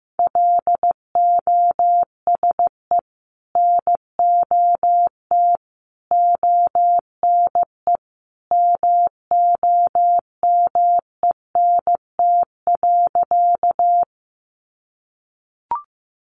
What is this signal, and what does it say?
Morse code, heard as a keyed tone, about 15 wpm, 700 Hz. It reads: LOSE NOT ONE MOMENT.
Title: Sentences from Top 1100 Words 15wpm